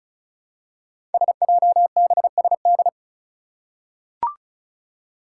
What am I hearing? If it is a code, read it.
Morse code, heard as a keyed tone, about 36 wpm, 700 Hz. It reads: SJBSD
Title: Five Challenging Letter-Number Combinations 35wpm